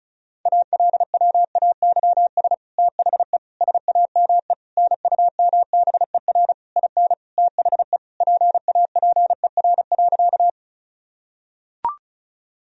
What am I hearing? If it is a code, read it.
Morse code, heard as a keyed tone, about 36 wpm, 700 Hz. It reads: ALWAYS THE SAME NUMBER IN THE PAPER.
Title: Sentences from Top 1000 Words - Review Part 1 - 35wpm